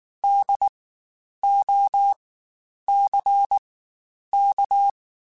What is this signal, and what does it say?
Morse code, heard as a keyed tone, about 19 wpm, spaced as for 9 wpm, 800 Hz. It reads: DOCK
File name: Spaced Alphas / Spaced Alphas mp3s / Ds sp @18